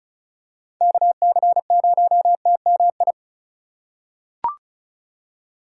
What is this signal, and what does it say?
Morse code, heard as a keyed tone, about 36 wpm, 700 Hz. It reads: KC0TMI
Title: USA Call Signs - 6 Characters - 2x Character Spacing - 35wpm